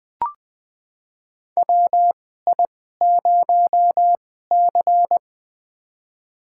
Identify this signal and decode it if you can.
Morse code, heard as a keyed tone, about 21 wpm, spaced as for 15 wpm, 700 Hz. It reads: WI0C
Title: USA Call Signs - 4 Characters - 2x Character Spacing - 20wpm